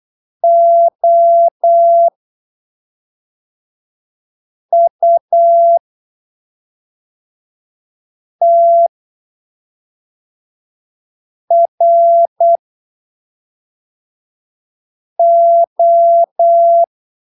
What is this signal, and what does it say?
Morse code, heard as a keyed tone, about 8 wpm, spaced as for 3 wpm, 680 Hz. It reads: OUTRO
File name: arquivos / texto / texto 15 08ppm